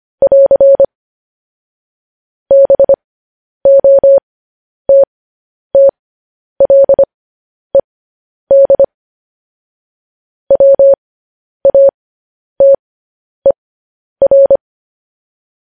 Morse code, heard as a keyed tone, about 25 wpm, spaced as for 10 wpm, 550 Hz. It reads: <AR> BOTTLED WATER
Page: News Headlines in Morse Code at 10 WPM